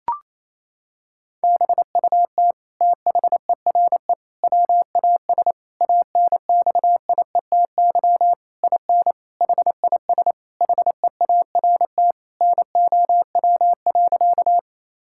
Morse code, heard as a keyed tone, about 27 wpm, 700 Hz. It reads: BUT THERE WAS ANXIETY IN HIS HEART NOW.
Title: Sentences from Top 1600 Words 28wpm